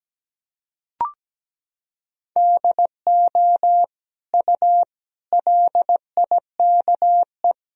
Morse code, heard as a keyed tone, about 17 wpm, 700 Hz. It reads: DO U LIKE
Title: Sentences from Top 100 Words and Abbreviations 17wpm